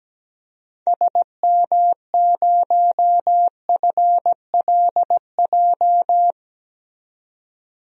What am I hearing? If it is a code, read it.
Morse code, heard as a keyed tone, about 17 wpm, 700 Hz. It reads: SM0FLJ
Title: DXCC-100 Call Signs - 6 Characters - 17wpm